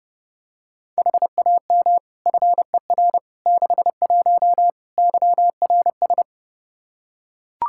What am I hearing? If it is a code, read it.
Morse code, heard as a keyed tone, about 30 wpm, 700 Hz. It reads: HAM FER 61 YRS
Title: K1USN SST QSO Practice 30wpm